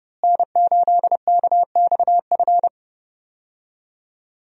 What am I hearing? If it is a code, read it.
Morse code, heard as a keyed tone, about 30 wpm, 700 Hz. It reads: N8KXF
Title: USA Call Signs - 5 Characters - Repeated 3x - 30wpm